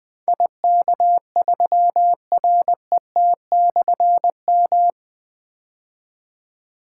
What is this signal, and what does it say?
Morse code, heard as a keyed tone, about 20 wpm, 700 Hz. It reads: IK3RET/M
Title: Difficult Call Signs 20wpm